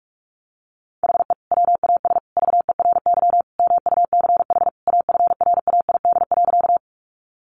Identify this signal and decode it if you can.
Morse code, heard as a keyed tone, about 44 wpm, 700 Hz. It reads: HE WAS VERY MUCH AFRAID.
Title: Sentences from Top 1500 Words 45wpm